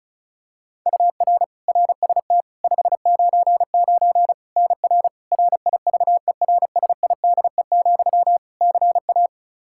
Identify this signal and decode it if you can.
Morse code, heard as a keyed tone, about 36 wpm, 700 Hz. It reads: UR RST 599 NR RIVERSIDE, CA
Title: Practice QSO - Exchange 1 - Beginning - 35wpm